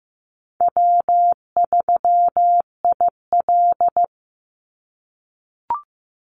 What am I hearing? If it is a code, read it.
Morse code, heard as a keyed tone, about 15 wpm, 700 Hz. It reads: W3IL
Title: Easy Call Signs - 15wpm